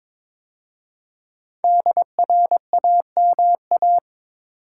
Morse code, heard as a keyed tone, about 22 wpm, 700 Hz. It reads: DRAMA
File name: 5 Letter Words - First 3 Letters Given - 22wpm